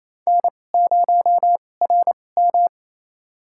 Morse code, heard as a keyed tone, about 28 wpm, spaced as for 20 wpm, 700 Hz. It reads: N0RM
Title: USA Call Signs - 4 Characters - 2x Character Spacing - 28wpm